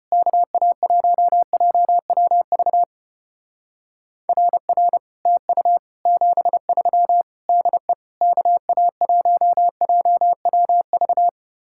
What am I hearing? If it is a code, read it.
Morse code, heard as a keyed tone, about 32 wpm, 700 Hz. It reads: KA1JWV RR TU 73 DE KA1JWV